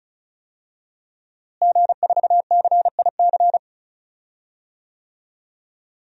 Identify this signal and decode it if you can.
Morse code, heard as a keyed tone, about 36 wpm, 700 Hz. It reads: G4CIC
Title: DXCC-100 Call Signs - 5 Characters - Repeated 3x - 35wpm